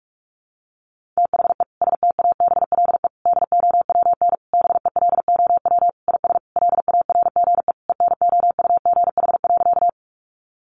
Morse code, heard as a keyed tone, about 44 wpm, 700 Hz. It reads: THE STABLE DOWN BELOW IS LARGE ENOUGH.
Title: Sentences from Top 1500 Words 45wpm